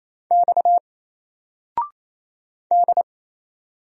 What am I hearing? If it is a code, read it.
Morse code, heard as a keyed tone, about 27 wpm, 700 Hz. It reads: X D
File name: Single-Letter-Number-28wpm